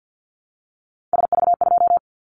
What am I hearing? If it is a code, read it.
Morse code, heard as a keyed tone, about 50 wpm, 700 Hz. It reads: SV2
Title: Callsign Prefixes for Top 50 DXCC - 50wpm